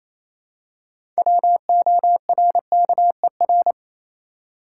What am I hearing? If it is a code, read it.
Morse code, heard as a keyed tone, about 28 wpm, 700 Hz. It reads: WORKER
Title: Six Letter Words - 28wpm